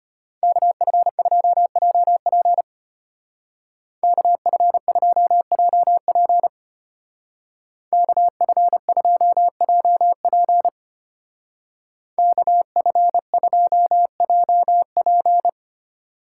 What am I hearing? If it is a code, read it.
Morse code, heard as a keyed tone, about 32 wpm, 700 Hz. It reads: KF2JP KF2JP KF2JP KF2JP